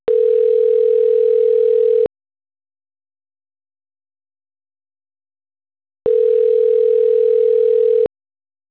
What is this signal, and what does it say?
Ringback tone (North America), 440 + 480 Hz.